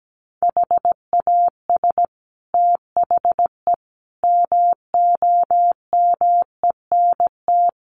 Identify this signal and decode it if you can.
Morse code, heard as a keyed tone, about 17 wpm, 700 Hz. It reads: HAS THE MOMENT
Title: Sentences from Top 1000 Words 17wpm